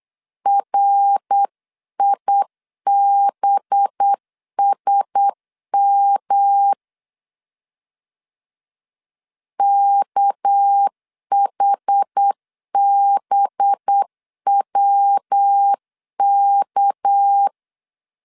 Morse code, heard as a keyed tone, about 8 wpm, 800 Hz. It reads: RIBSM KHBWK
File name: CW / DET / 1 6 1500C S B 8wpm 5142